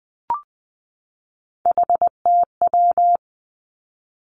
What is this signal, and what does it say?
Morse code, heard as a keyed tone, about 20 wpm, 700 Hz. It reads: HTW
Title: Three Challenging Letter-Number Combinations 20wpm